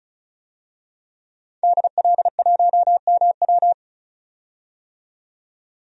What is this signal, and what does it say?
Morse code, heard as a keyed tone, about 34 wpm, 700 Hz. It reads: DL1MW
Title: DXCC-100 Call Signs - 5 Characters - 35wpm